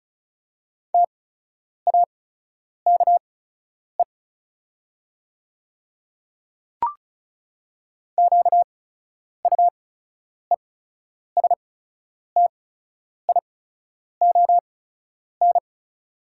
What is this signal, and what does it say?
Morse code, heard as a keyed tone, about 36 wpm, spaced as for 10 wpm, 700 Hz. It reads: TAKE QUESTION
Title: 8x Character Spacing - Top 500 Words - 35wpm